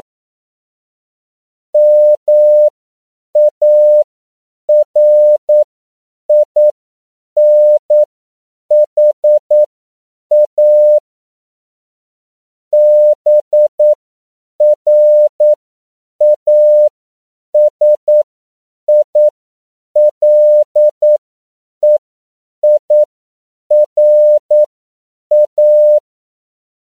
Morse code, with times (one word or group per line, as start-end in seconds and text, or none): 1.7-11.0 MARINHA
12.7-26.0 BRASILEIRA